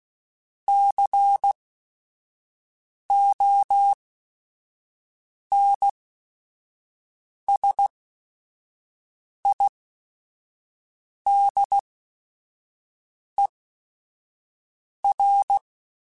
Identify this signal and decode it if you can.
Morse code, heard as a keyed tone, about 16 wpm, spaced as for 5 wpm, 800 Hz. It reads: CONSIDER
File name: Technical Subjects / Technical Subjects-mp3s / OR Gate @5